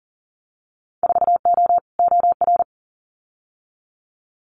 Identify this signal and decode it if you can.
Morse code, heard as a keyed tone, about 39 wpm, 700 Hz. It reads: 4O OR
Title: Winter Field Day Exchanges Encore 40wpm